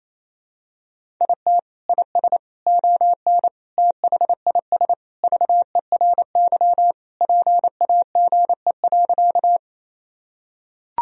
Morse code, heard as a keyed tone, about 28 wpm, 700 Hz. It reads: IT IS ON THIS VERY PAGE.
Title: Sentences from Top 1000 Words - Review Part 1 - 28wpm